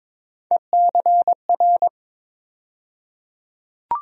Morse code, heard as a keyed tone, about 22 wpm, 700 Hz. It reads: ECR
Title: USA Call Sign Postfixes - 3 Letters 22wpm